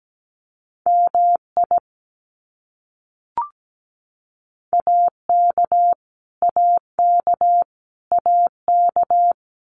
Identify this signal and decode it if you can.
Morse code, heard as a keyed tone, about 17 wpm, 700 Hz. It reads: MI AK AK AK
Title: US States Repeated 3x 17wpm